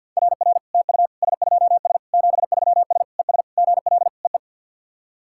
Morse code, heard as a keyed tone, about 54 wpm, 700 Hz. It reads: RR TU S2S 73S ES GL EE
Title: SOTA QSO Practice 50wpm